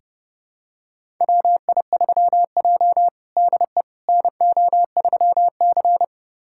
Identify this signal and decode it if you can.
Morse code, heard as a keyed tone, about 30 wpm, 700 Hz. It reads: WI3J DE NO3C
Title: USA Callsign DE Callsign - 4 Characters - 30wpm